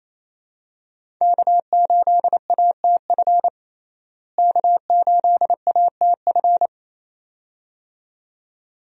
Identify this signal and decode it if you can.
Morse code, heard as a keyed tone, about 28 wpm, 700 Hz. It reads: K8ATF K8ATF
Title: USA Call Signs - 5 Characters - Repeated 2x - 28wpm